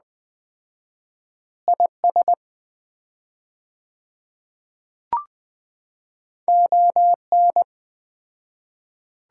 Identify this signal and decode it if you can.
Morse code, heard as a keyed tone, about 20 wpm, 700 Hz. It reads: IS ON
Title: 4x Word-Spacing - Sets of 4 Words - 20wpm